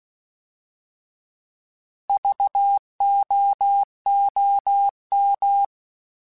Morse code, heard as a keyed tone, about 16 wpm, 800 Hz. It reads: VOOM